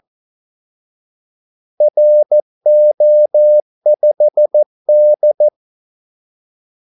Morse code, heard as a keyed tone, about 14 wpm, 600 Hz. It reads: RO5D
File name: Callsigns-14wpm